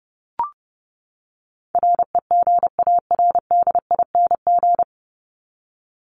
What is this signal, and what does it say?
Morse code, heard as a keyed tone, about 30 wpm, 700 Hz. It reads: REGARDING
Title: ING Suffix - Top 100 Words with Base Words 30wpm